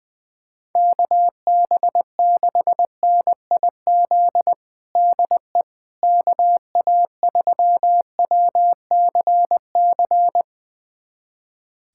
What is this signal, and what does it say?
Morse code, heard as a keyed tone, about 20 wpm, 700 Hz. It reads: KB6NIZ DE KA3WCC
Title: USA Callsign DE Callsign - 6 Characters - 20wpm